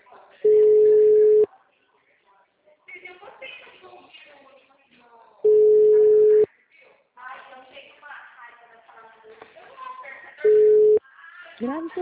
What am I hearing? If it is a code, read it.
Ringback tone (Europe, 425 Hz), with speech.